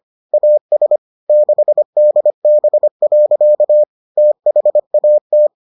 Morse code, heard as a keyed tone, about 25 wpm, 600 Hz. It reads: AS 6DB. THAT